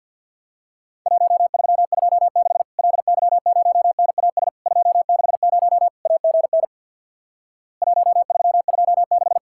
Morse code, heard as a keyed tone, about 50 wpm, 700 Hz. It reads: 1326 LY0NAS 260 AGN 1326
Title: Worked All Europe DX Contest - QTC Exchange with Repeats - 50wpm